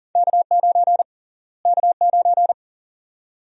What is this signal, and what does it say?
Morse code, heard as a keyed tone, about 40 wpm, 700 Hz. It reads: K9 K9
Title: USA Call Sign Prefixes - Repeated 2x - 40wpm